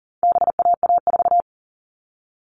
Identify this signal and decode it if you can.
Morse code, heard as a keyed tone, about 40 wpm, 700 Hz. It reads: BAA4
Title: All Cut Numbers - 4-Digit 40wpm